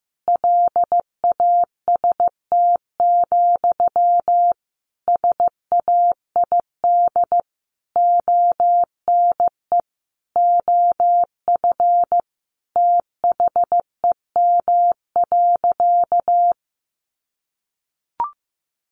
Morse code, heard as a keyed tone, about 15 wpm, 700 Hz. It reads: LAST, SAID ONE OF THEM.
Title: Sentences from Top 300 Words 15wpm